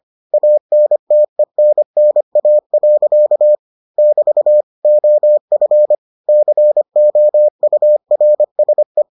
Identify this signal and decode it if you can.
Morse code, heard as a keyed tone, about 25 wpm, 600 Hz. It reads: ANTENNA. <BT> OF COURSE